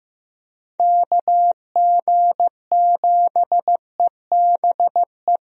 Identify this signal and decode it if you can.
Morse code, heard as a keyed tone, about 15 wpm, 700 Hz. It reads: KG7EBE
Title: USA Call Signs - 6 Characters - Repeated 3x - 15wpm